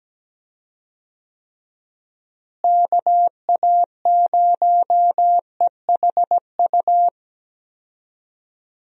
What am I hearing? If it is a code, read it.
Morse code, heard as a keyed tone, about 17 wpm, 700 Hz. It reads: KA0EHU
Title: USA Call Signs - 6 Characters - Repeated 3x - 17wpm